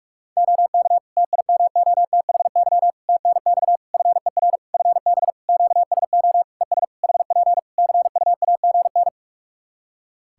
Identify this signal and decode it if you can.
Morse code, heard as a keyed tone, about 42 wpm, 700 Hz. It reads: OK TIMOTHY TNX FER FB QSO ES HP CUAGN
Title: K1USN SST QSO Practice 45wpm